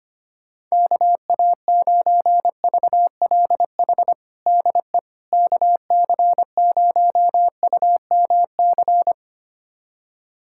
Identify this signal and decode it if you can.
Morse code, heard as a keyed tone, about 25 wpm, 700 Hz. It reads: KA9VLH DE KC0UMC
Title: USA Callsign DE Callsign - 6 Characters - 25wpm